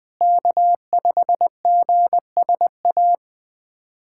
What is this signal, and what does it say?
Morse code, heard as a keyed tone, about 21 wpm, 700 Hz. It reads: K5GSA
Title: USA Callsign DE Callsign 20wpm